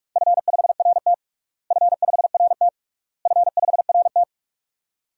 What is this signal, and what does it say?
Morse code, heard as a keyed tone, about 46 wpm, 700 Hz. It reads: F5RT F5RT F5RT
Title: DXCC-100 Call Signs - 4 Characters - Repeated 3x - 45wpm